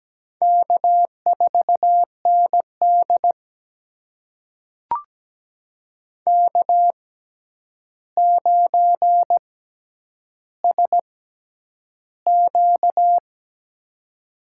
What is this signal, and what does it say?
Morse code, heard as a keyed tone, about 17 wpm, spaced as for 9 wpm, 700 Hz. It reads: K4ND K9SQ
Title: USA Call Signs - 4 Characters - 6x Character Spacing - 17wpm